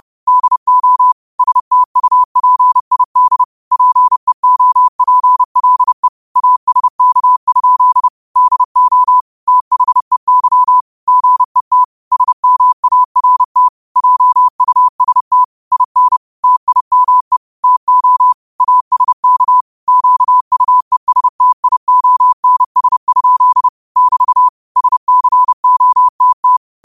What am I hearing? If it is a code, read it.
Morse code, heard as a keyed tone, about 30 wpm, 1000 Hz. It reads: DO STUPID PEOPLE ASK? DO THEY GET SMART JUST IN TIME TO ASK QUESTIONS? <BT> SCOTT